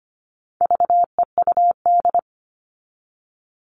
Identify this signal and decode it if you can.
Morse code, heard as a keyed tone, about 25 wpm, 700 Hz. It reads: VEUD
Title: All Cut Numbers - 4-Digit 25wpm